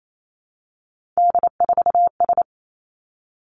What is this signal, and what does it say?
Morse code, heard as a keyed tone, about 28 wpm, 700 Hz. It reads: D4S
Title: Three Challenging Letter-Number Combinations 28wpm